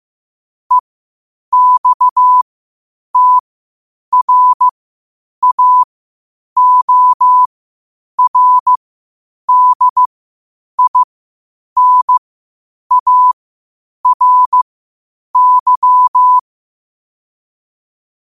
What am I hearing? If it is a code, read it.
Morse code, heard as a keyed tone, about 15 wpm, spaced as for 8 wpm, 1000 Hz. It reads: EXTRAORDINARY
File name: qotdfiles / 8wpm